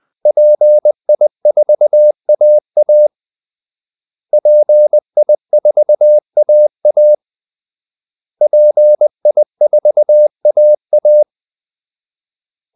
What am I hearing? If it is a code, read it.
Morse code, heard as a keyed tone, about 20 wpm, 600 Hz. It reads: PI4AA PI4AA PI4AA
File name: PI4AA Uitzending